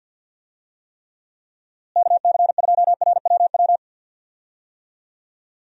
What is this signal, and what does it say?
Morse code, heard as a keyed tone, about 47 wpm, 700 Hz. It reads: KC2RWW
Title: USA Call Signs - 6 Characters - Repeated 2x - 50wpm